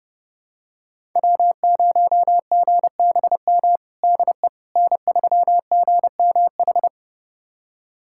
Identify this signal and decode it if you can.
Morse code, heard as a keyed tone, about 32 wpm, 700 Hz. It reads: W0GBM DE N3GMH
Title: USA Callsign DE Callsign - 5 Characters - 30wpm